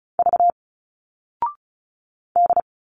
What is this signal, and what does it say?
Morse code, heard as a keyed tone, about 36 wpm, 700 Hz. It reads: V D
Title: Single Letters 35wpm